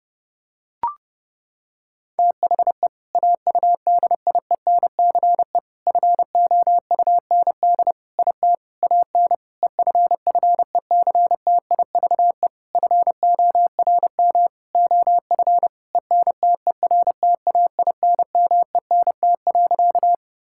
Morse code, heard as a keyed tone, about 32 wpm, 700 Hz. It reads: THE AUDIENCE FOUND IT AN EFFECTIVE FORM OF ENTERTAINMENT.
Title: Sentences from Top 1000 Words - Review Part 6 - 30wpm